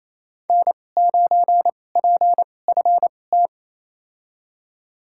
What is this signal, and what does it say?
Morse code, heard as a keyed tone, about 28 wpm, spaced as for 20 wpm, 700 Hz. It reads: N9PFT
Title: USA Call Signs - 5 Characters - 2x Character Spacing - 28wpm